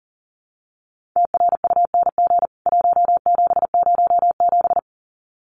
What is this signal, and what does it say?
Morse code, heard as a keyed tone, about 40 wpm, 700 Hz. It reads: TRUNG 1707
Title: CWT Exchange 40wpm